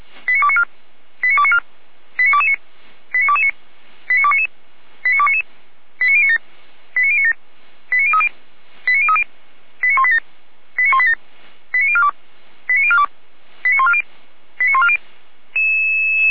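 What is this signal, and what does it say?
Five-tone selective call (ZVEI), heard as quick runs of five tones, tones of about 80 ms. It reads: 89294 89294 89229 89229 89299 89299 89097 89097 89030 89030 89187 89187 89042 89042 89140 89140
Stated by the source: SDR-received ZVEI five-tone call, code 89276, archived by feuerwehr-orlamuende (fire brigade Orlamünde) server f85360.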